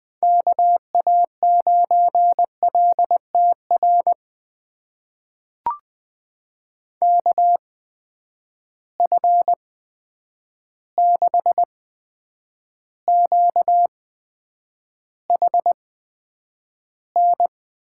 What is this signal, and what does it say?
Morse code, heard as a keyed tone, about 20 wpm, spaced as for 9 wpm, 700 Hz. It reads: KA9LTR KF6QHN
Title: USA Call Signs - 6 Characters - 8x Character Spacing - 20wpm